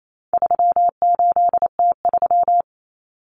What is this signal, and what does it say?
Morse code, heard as a keyed tone, about 28 wpm, 700 Hz. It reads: 38T3
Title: Common Cut Numbers - 4-Digit 28wpm